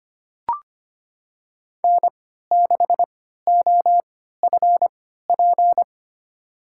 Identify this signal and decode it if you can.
Morse code, heard as a keyed tone, about 25 wpm, spaced as for 14 wpm, 700 Hz. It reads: N6OFP
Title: USA Call Signs - 5 Characters - 3x Character Spacing - 25wpm